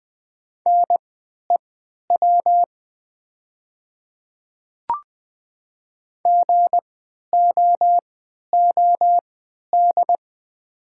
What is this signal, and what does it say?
Morse code, heard as a keyed tone, about 20 wpm, spaced as for 11 wpm, 700 Hz. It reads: NEW GOOD
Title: 3x Character Spacing - Top 300 Words - 20wpm